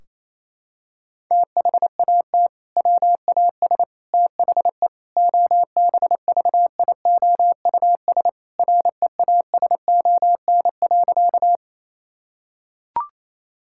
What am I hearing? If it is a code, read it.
Morse code, heard as a keyed tone, about 28 wpm, 700 Hz. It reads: THAT WAS THE OBVIOUS REASON.
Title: Sentences from Top 1600 Words 28wpm